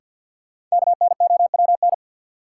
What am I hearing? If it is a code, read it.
Morse code, heard as a keyed tone, about 50 wpm, 700 Hz. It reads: KNOWN
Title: Sentences from Top 600 Words - New Word Drill - 50wpm